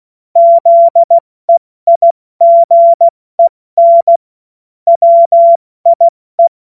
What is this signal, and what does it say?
Morse code, heard as a keyed tone, about 16 wpm, 680 Hz. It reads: ZEIGEN WIE
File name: Morsekurs / Witze / Witz 09 Januar 2024 70 F80